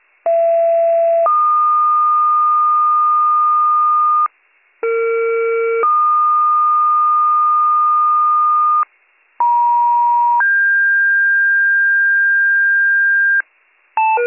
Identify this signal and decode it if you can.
Two-tone page, heard as a one-second tone, then a three-second tone, about 670 then 1220 Hz, about 460 then 1220 Hz, about 950 then 1600 Hz. It is heard from 0.3 to 13.4 s.